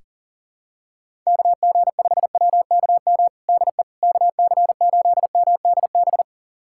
Morse code, heard as a keyed tone, about 40 wpm, 700 Hz. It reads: KG5WKM DE KC8MDB
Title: USA Callsign DE Callsign 40wpm